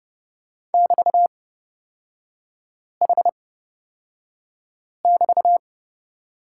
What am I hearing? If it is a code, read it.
Morse code, heard as a keyed tone, about 29 wpm, spaced as for 5 wpm, 700 Hz. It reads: <BT>H<BT>
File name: Lesson 36e - BT - Three Characters 30wpm